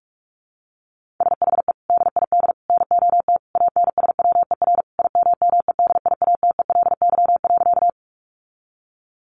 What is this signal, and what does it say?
Morse code, heard as a keyed tone, about 44 wpm, 700 Hz. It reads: SHE DID NOT ANSWER IMMEDIATELY.
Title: Sentences from Top 1100 Words 45wpm